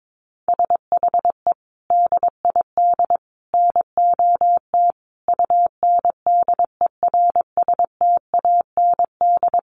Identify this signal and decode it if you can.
Morse code, heard as a keyed tone, about 22 wpm, 700 Hz. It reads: SHE DID NOT UNDERSTAND
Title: Sentences from Top 600 Words 22wpm